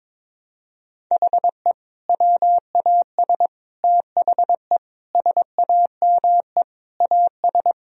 Morse code, heard as a keyed tone, about 22 wpm, 700 Hz. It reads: HE WAS THE SAME AS
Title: Sentences from Top 1000 Words - Review Part 2 - 22wpm